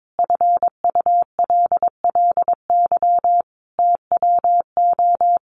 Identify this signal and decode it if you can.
Morse code, heard as a keyed tone, about 22 wpm, 700 Hz. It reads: FULLY TWO
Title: Sentences from Top 1300 Words 22wpm